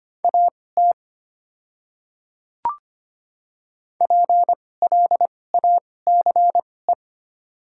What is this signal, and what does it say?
Morse code, heard as a keyed tone, about 25 wpm, spaced as for 18 wpm, 690 Hz. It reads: AT PLACE
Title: 2x Character Spacing - Top 200 Words - 25wpm